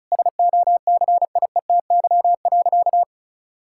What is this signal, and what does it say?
Morse code, heard as a keyed tone, about 36 wpm, 700 Hz. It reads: SOCIETY.
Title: Sentences from Top 1000 Words - Review Part 6 - 35wpm